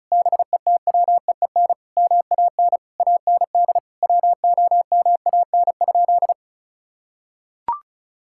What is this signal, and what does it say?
Morse code, heard as a keyed tone, about 34 wpm, 700 Hz. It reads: BETWEEN MAN AND WOMAN?
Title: Sentences from Top 1600 Words 35wpm